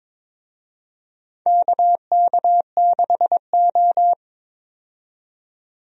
Morse code, heard as a keyed tone, about 22 wpm, 700 Hz. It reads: KK6O
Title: USA Call Signs - 4 Characters - 22wpm